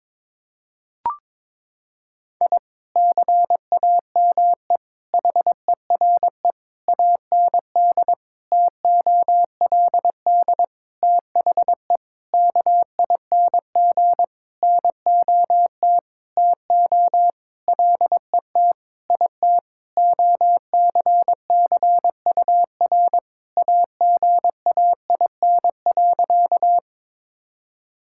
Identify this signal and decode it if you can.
Morse code, heard as a keyed tone, about 22 wpm, 700 Hz. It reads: I CAME HERE AND TOLD THE KING NOT TO LET IT OCCUR AGAIN.
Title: Sentences from Top 1000 Words - Review Part 6 - 22wpm